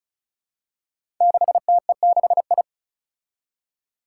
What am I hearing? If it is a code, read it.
Morse code, heard as a keyed tone, about 36 wpm, 700 Hz. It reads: 6TE6I
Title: Five Challenging Letter-Number Combinations 35wpm